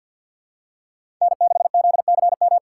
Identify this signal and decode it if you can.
Morse code, heard as a keyed tone, about 52 wpm, 700 Hz. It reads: N6ZCM
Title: USA Call Signs - 5 Characters - 3x Character Spacing - 50wpm